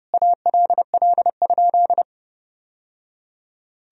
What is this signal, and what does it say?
Morse code, heard as a keyed tone, about 30 wpm, 700 Hz. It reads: ALL?
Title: Sentences from Top 1000 Words - Review Part 3 - 30wpm